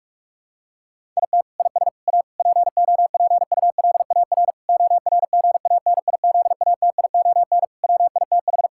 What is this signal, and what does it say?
Morse code, heard as a keyed tone, about 46 wpm, 700 Hz. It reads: IT IS A POPULAR ORGANIZATION WITH
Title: Sentences from Top 1000 Words - Review Part 5 - 45wpm